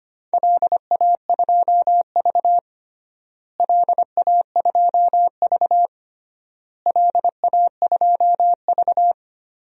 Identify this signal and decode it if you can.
Morse code, heard as a keyed tone, about 25 wpm, 700 Hz. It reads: LA2V LA2V LA2V
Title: DXCC-100 Call Signs - 4 Characters - Repeated 3x - 25wpm